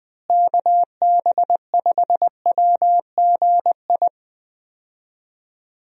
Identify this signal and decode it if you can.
Morse code, heard as a keyed tone, about 20 wpm, 700 Hz. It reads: KB5WGI
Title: USA Call Signs - 6 Characters - Repeated 3x - 20wpm